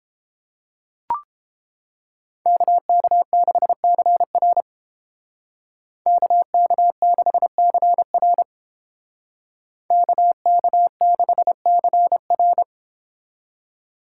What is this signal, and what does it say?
Morse code, heard as a keyed tone, about 30 wpm, 700 Hz. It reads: KK6CR KK6CR KK6CR